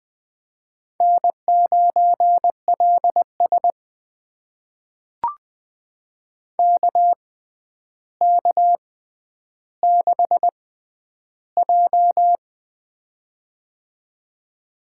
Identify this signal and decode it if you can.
Morse code, heard as a keyed tone, about 20 wpm, spaced as for 10 wpm, 700 Hz. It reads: N9LS KK6J
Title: USA Call Signs - 4 Characters - 6x Character Spacing - 20wpm